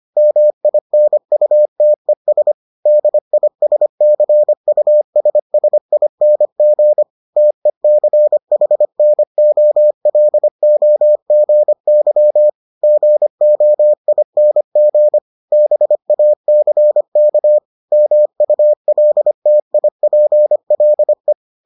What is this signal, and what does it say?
Morse code, heard as a keyed tone, about 25 wpm, 600 Hz. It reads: MINUTES DISCUSSING TECHNOLOGY GOING BACK MULTIPLE